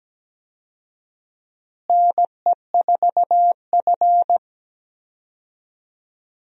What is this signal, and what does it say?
Morse code, heard as a keyed tone, about 17 wpm, 700 Hz. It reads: NE4F
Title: USA Call Signs - 4 Characters - Repeated 3x - 17wpm